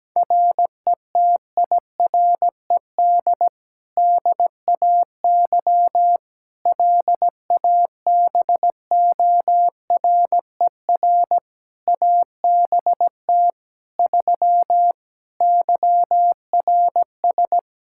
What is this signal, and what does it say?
Morse code, heard as a keyed tone, about 17 wpm, 700 Hz. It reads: RETIRED DAY LABORER ABT 3 YRS
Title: Practice QSO - Exchange 3 - More Info - 17wpm